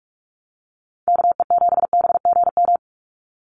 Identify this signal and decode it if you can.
Morse code, heard as a keyed tone, about 44 wpm, 700 Hz. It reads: KE7BGM
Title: NA SOTA Activator Callsigns 45wpm